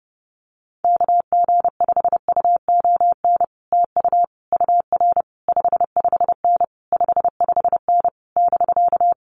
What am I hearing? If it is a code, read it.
Morse code, heard as a keyed tone, about 28 wpm, 700 Hz. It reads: KG5UON TU UR 55N 55N <BK>
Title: Common QSO Phrases Encore 30wpm